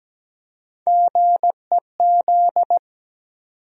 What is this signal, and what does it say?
Morse code, heard as a keyed tone, about 17 wpm, 700 Hz. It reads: GEZ
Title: USA Call Sign Postfixes - 3 Letters 17wpm